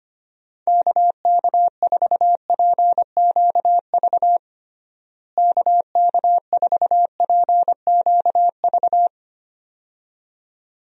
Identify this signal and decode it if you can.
Morse code, heard as a keyed tone, about 25 wpm, 700 Hz. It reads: KK4PQV KK4PQV
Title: USA Call Signs - 6 Characters - Repeated 2x - 25wpm